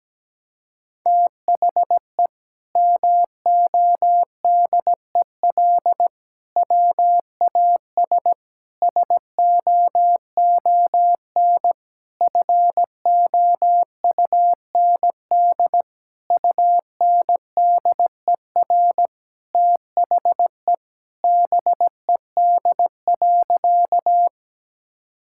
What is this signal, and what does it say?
Morse code, heard as a keyed tone, about 17 wpm, 700 Hz. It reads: THE MODEL WAS SOON FOUND UNDER THE BED.
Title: Sentences from Top 1000 Words - Review Part 5 - 17wpm